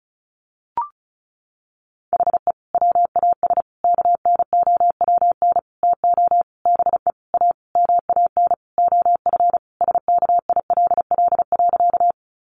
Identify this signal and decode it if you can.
Morse code, heard as a keyed tone, about 34 wpm, 700 Hz. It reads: HE WAS KNOWN TO BE A MAN OF SKILL.
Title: Sentences from Top 1200 Words 35wpm